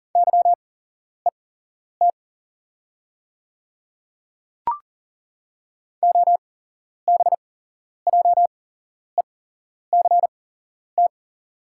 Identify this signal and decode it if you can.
Morse code, heard as a keyed tone, about 40 wpm, spaced as for 11 wpm, 690 Hz. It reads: YET OBJECT
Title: 8x Character Spacing - Top 500 Words - 40wpm